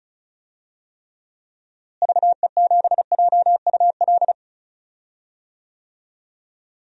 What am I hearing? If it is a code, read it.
Morse code, heard as a keyed tone, about 34 wpm, 700 Hz. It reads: VE7JUL
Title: DXCC-100 Call Signs - 6 Characters - Repeated 3x - 35wpm